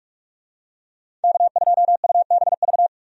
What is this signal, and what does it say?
Morse code, heard as a keyed tone, about 45 wpm, 700 Hz. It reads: K2UBV
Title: USA Call Signs - 5 Characters - 45wpm